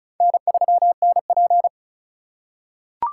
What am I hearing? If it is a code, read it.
Morse code, heard as a keyed tone, about 34 wpm, 700 Hz. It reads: N3NP
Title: USA Call Signs - 4 Characters - 4x Character Spacing - 35wpm